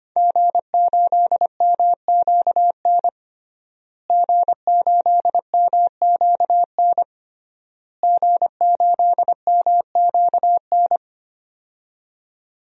Morse code, heard as a keyed tone, about 25 wpm, 700 Hz. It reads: G8MQN G8MQN G8MQN
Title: DXCC-100 Call Signs - 5 Characters - Repeated 3x - 25wpm